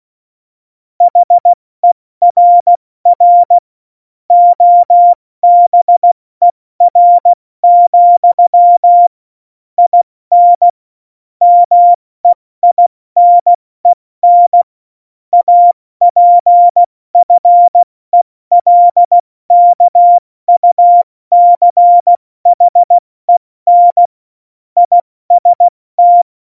Morse code, heard as a keyed tone, about 16 wpm, 700 Hz. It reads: HERR OBER, IN MEINEN APFELKUCHEN IST